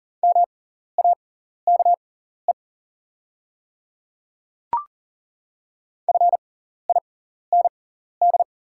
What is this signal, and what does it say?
Morse code, heard as a keyed tone, about 40 wpm, spaced as for 14 wpm, 690 Hz. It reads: MAKE FIND
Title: 6x Character Spacing - Top 100 Words - 40wpm